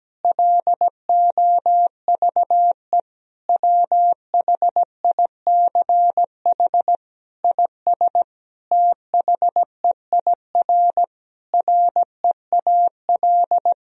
Morse code, heard as a keyed tone, about 17 wpm, 700 Hz. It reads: LOVE WHICH IS THEIR REAL
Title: Sentences from Top 1000 Words - Review Part 1 - 17wpm